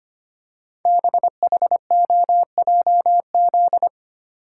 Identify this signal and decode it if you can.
Morse code, heard as a keyed tone, about 25 wpm, 700 Hz. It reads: BHOJZ
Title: Code Groups - 5 Letters - 25wpm